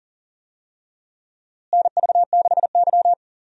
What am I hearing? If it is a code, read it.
Morse code, heard as a keyed tone, about 39 wpm, 700 Hz. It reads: NV6Y
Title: USA Call Signs - 4 Characters - Repeated 3x - 40wpm